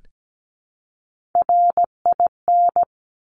Morse code, heard as a keyed tone, about 17 wpm, 700 Hz. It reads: RIN
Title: Top Three Letter Combinations in Words 17wpm